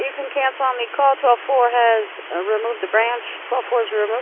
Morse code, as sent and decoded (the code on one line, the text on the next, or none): none